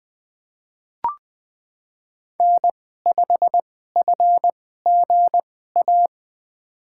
Morse code, heard as a keyed tone, about 20 wpm, spaced as for 14 wpm, 700 Hz. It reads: N5FGA
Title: USA Call Signs - 5 Characters - 2x Character Spacing - 20wpm